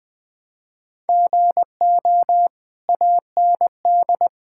Morse code, heard as a keyed tone, about 20 wpm, 700 Hz. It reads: GO AND
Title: Sentences from Top 1000 Words - Review Part 6 - 20wpm